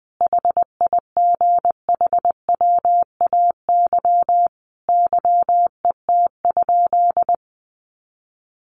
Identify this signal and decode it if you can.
Morse code, heard as a keyed tone, about 19 wpm, 700 Hz. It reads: HIGHWAY YET?
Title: Sentences from Top 1400 Words 20wpm